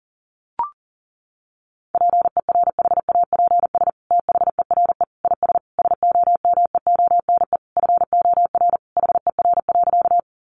Morse code, heard as a keyed tone, about 40 wpm, 700 Hz. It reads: PERHAPS THERE IS SOMEONE FOR HER.
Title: Sentences from Top 600 Words 40wpm